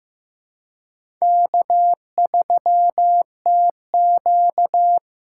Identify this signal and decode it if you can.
Morse code, heard as a keyed tone, about 15 wpm, 700 Hz. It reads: K3TQ
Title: USA Call Signs - 4 Characters - 3x Character Spacing - 15wpm